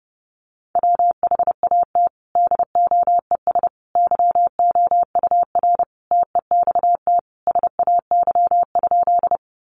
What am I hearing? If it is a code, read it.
Morse code, heard as a keyed tone, about 30 wpm, 700 Hz. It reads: WHAT DOES YOUR TEXT SAY?
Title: Sentences from Top 1200 Words 30wpm